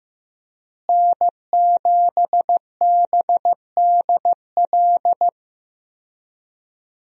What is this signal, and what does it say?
Morse code, heard as a keyed tone, about 15 wpm, 700 Hz. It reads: N7BDL
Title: USA Call Signs - 5 Characters - Repeated 3x - 15wpm